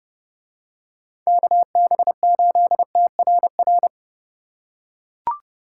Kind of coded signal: Morse code

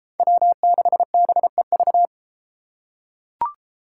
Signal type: Morse code

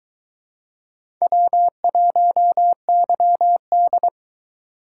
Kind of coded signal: Morse code

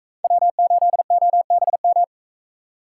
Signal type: Morse code

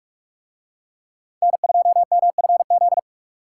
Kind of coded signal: Morse code

